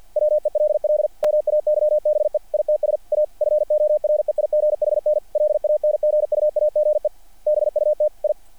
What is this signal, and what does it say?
Morse code, heard as a keyed tone, about 48 wpm, 660 Hz. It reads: WELL MAYBE ITS A FOREIGHN LANGUAGE BUT I
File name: archived / CRSnet